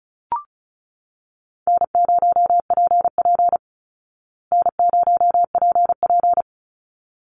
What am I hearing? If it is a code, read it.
Morse code, heard as a keyed tone, about 34 wpm, 700 Hz. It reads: N0PP N0PP
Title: Easy Call Signs Repeated 2x - 35wpm